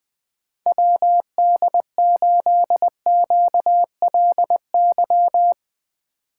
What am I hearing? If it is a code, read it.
Morse code, heard as a keyed tone, about 20 wpm, 700 Hz. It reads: WD8QLY